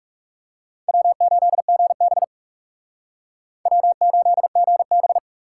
Morse code, heard as a keyed tone, about 37 wpm, 700 Hz. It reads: W8GB W8GB
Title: Speed Racing - USA 1x2 Call Signs 30wpm